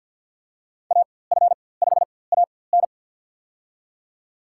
Morse code, heard as a keyed tone, about 49 wpm, spaced as for 23 wpm, 700 Hz. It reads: AF5AN
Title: USA Call Signs - 5 Characters - 4x Character Spacing - 50wpm